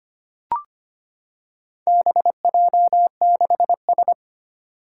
Morse code, heard as a keyed tone, about 25 wpm, 700 Hz. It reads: BJ6S
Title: Four Challenging Letter-Number Combinations 25wpm